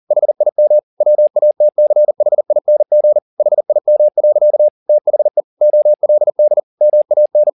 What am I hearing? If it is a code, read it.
Morse code, heard as a keyed tone, about 40 wpm, 600 Hz. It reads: HIM WATCHING HIM. THE OLD MAN
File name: the old man and the sea 2 40wpm